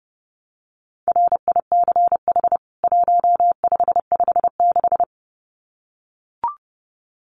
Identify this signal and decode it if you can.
Morse code, heard as a keyed tone, about 30 wpm, 700 Hz. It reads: RICH 1556
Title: CWT Exchange 30wpm